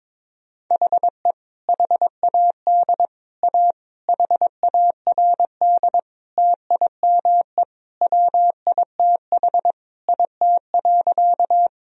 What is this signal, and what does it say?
Morse code, heard as a keyed tone, about 22 wpm, 700 Hz. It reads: HE HAD A HARD TIME WITH IT.
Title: Sentences from Top 1000 Words - Review Part 1 - 22wpm